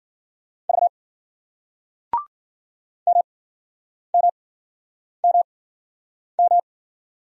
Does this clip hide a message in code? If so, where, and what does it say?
Yes: Morse code from 0.7 to 6.6 s, heard as a keyed tone, about 54 wpm, spaced as for 9 wpm, 700 Hz. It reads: V MMMM